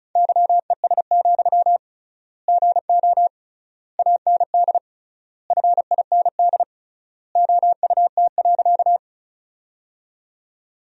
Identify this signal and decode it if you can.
Morse code, heard as a keyed tone, about 34 wpm, 700 Hz. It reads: YES, GO AND FIND OUT.
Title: 3x Word-Spacing - Sentences from Top 500 Words - 35wpm